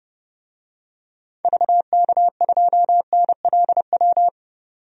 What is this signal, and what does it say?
Morse code, heard as a keyed tone, about 30 wpm, 700 Hz. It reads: VK2NLW
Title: DXCC-100 Call Signs - 6 Characters - 30wpm